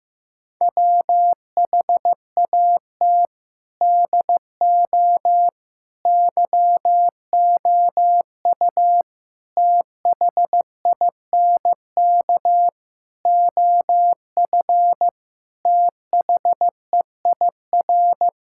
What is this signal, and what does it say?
Morse code, heard as a keyed tone, about 15 wpm, 700 Hz. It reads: WHAT DO YOU THINK OF THEIR
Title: Sentences from Top 1000 Words - Review Part 5 - 15wpm